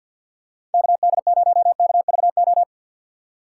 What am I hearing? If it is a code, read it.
Morse code, heard as a keyed tone, about 52 wpm, 700 Hz. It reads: KD0KVO
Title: USA Call Signs - 6 Characters - 6x Character Spacing - 50wpm